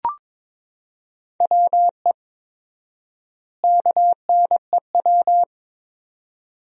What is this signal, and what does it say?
Morse code, heard as a keyed tone, about 22 wpm, 700 Hz. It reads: WE KNEW
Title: 4x Word-Spacing - Sentences from Top 400 Words - 22wpm